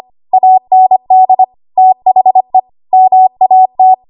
Morse code, heard as a keyed tone, about 25 wpm, 750 Hz. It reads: AND THE MAT
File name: Archive / 25 WPM / 25WPM